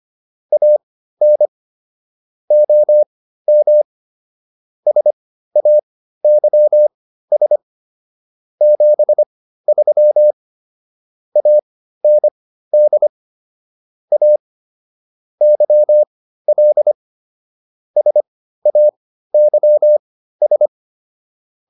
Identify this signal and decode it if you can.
Morse code, heard as a keyed tone, about 25 wpm, spaced as for 14 wpm, 600 Hz. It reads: AN OM SAYS 73 AND A YL SAYS